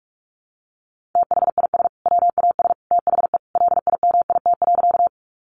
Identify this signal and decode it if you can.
Morse code, heard as a keyed tone, about 44 wpm, 700 Hz. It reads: THIS WAS THE LIMIT.
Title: Sentences from Top 1500 Words 45wpm